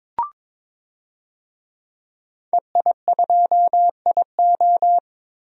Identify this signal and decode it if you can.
Morse code, heard as a keyed tone, about 21 wpm, 700 Hz. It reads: EI2IO
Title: DXCC-100 Call Signs - 5 Characters - 22wpm